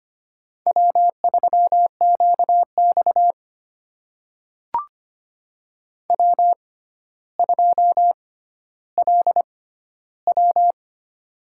Morse code, heard as a keyed tone, about 25 wpm, spaced as for 13 wpm, 700 Hz. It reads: W3QX W2LW
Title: USA Call Signs - 4 Characters - 6x Character Spacing - 25wpm